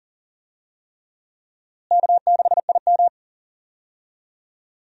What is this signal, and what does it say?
Morse code, heard as a keyed tone, about 40 wpm, 700 Hz. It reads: K6IM
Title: USA Call Signs - 4 Characters - Repeated 3x - 40wpm